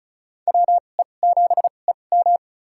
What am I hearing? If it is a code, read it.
Morse code, heard as a keyed tone, about 36 wpm, spaced as for 25 wpm, 700 Hz. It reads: WE7EM
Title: USA Call Signs - 5 Characters - 2x Character Spacing - 35wpm